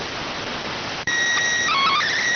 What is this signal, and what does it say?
SSTV (slow-scan television), PD 290.